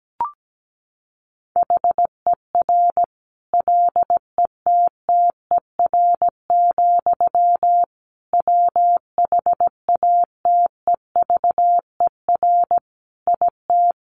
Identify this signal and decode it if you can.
Morse code, heard as a keyed tone, about 17 wpm, 700 Hz. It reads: HER LETTER, WHATEVER IT
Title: Sentences from Top 1100 Words 17wpm